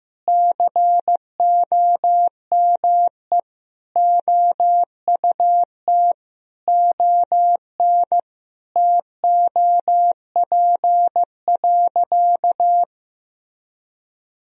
Morse code, heard as a keyed tone, about 15 wpm, 700 Hz. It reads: COME OUT ON TOP.
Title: Sentences from Top 1000 Words - Review Part 2 - 15wpm